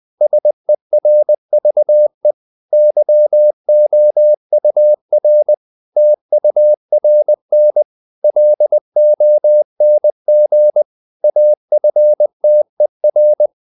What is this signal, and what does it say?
Morse code, heard as a keyed tone, about 20 wpm, 600 Hz. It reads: SERVE YOUR TURN LONG AFTER
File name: CW / Practice / If20